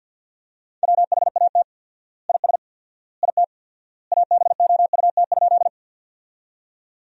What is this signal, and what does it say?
Morse code, heard as a keyed tone, about 50 wpm, 700 Hz. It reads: WHAT IS IT ABOUT?
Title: 4x Word-Spacing - Sentences from Top 100 Words - 50wpm